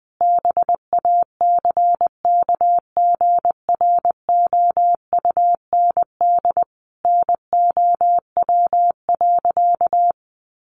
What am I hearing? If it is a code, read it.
Morse code, heard as a keyed tone, about 20 wpm, 700 Hz. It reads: BACKGROUND NOW.
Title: Sentences from Top 1300 Words 20wpm